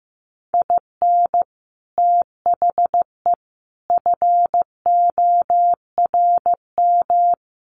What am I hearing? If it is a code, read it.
Morse code, heard as a keyed tone, about 15 wpm, 700 Hz. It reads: IN THE FORM
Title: Sets of 5 Words Encore2 15wpm